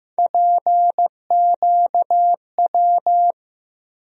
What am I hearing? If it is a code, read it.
Morse code, heard as a keyed tone, about 15 wpm, 700 Hz. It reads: PQW
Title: USA Call Sign Postfixes - 3 Letters 15wpm